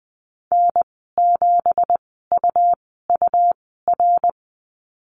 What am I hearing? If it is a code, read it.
Morse code, heard as a keyed tone, about 20 wpm, spaced as for 15 wpm, 700 Hz. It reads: N7UUR
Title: 167 Lesson 27g - Number 7 - Call Signs 20wpm